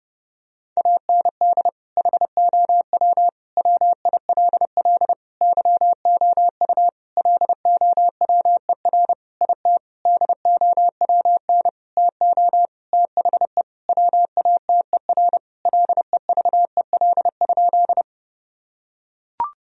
Morse code, heard as a keyed tone, about 30 wpm, 700 Hz. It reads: AND HOW WILL YOU LOWER IT DOWN TO THE WATER LEVEL?
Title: Sentences from Top 1000 Words - Review Part 4 - 30wpm